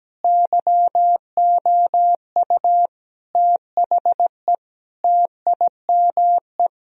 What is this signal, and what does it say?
Morse code, heard as a keyed tone, about 17 wpm, 700 Hz. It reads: YOU THE TIME
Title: Sentences from Top 1000 Words - Review Part 1 - 17wpm